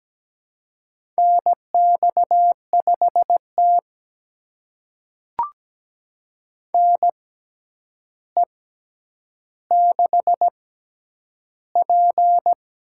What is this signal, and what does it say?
Morse code, heard as a keyed tone, about 17 wpm, spaced as for 9 wpm, 700 Hz. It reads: NX5T NE6P
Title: USA Call Signs - 4 Characters - 6x Character Spacing - 17wpm